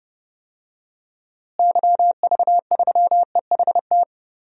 Morse code, heard as a keyed tone, about 30 wpm, 700 Hz. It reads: YV3EHT
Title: DXCC-100 Call Signs - 6 Characters - Repeated 3x - 30wpm